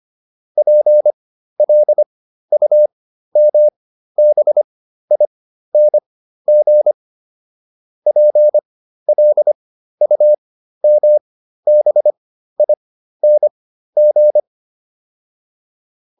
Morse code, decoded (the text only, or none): PLUMBING PLUMBING